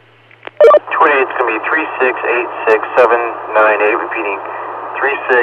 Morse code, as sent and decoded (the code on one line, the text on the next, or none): none